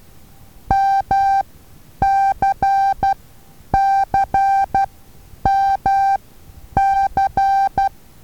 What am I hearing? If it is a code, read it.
Morse code, heard as a keyed tone, about 12 wpm, spaced as for 9 wpm, 780 Hz. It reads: MCCMC
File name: Taviro HA6VA A